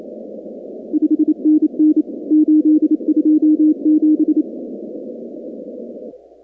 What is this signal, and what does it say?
Morse code, heard as a keyed tone, about 27 wpm, 310 Hz. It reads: 5NN 827
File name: contests / 2019-cw-wpx-kt5j-k5pi- -2330 0-19